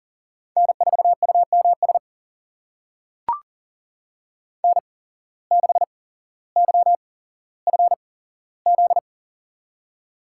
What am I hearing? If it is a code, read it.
Morse code, heard as a keyed tone, about 40 wpm, spaced as for 17 wpm, 700 Hz. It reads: N4UMS N6YFZ
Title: USA Call Signs - 8x Character Spacing - 40wpm